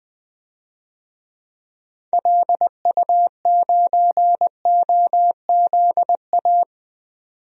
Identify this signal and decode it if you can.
Morse code, heard as a keyed tone, about 21 wpm, 700 Hz. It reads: LU9OZA
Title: DXCC-100 Call Signs - 6 Characters - 20wpm